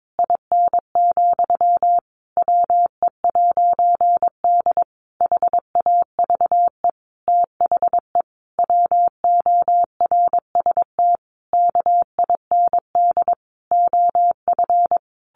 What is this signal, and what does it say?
Morse code, heard as a keyed tone, about 22 wpm, 700 Hz. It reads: IN, WE'D HAVE THE WORST KIND OF
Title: A Sound Of Thunder 22wpm